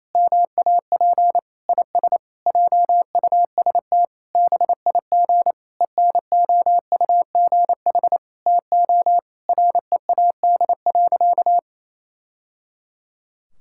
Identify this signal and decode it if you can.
Morse code, heard as a keyed tone, about 28 wpm, 700 Hz. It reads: MAP IS JUST BIG ENOUGH TO READ.
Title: Sentences from Top 1000 Words - Review Part 5 - 28wpm